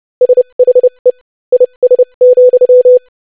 Morse code, heard as a keyed tone, about 30 wpm, spaced as for 25 wpm, 500 Hz. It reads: SHE IS,
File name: lesson / basicphrase-25wpm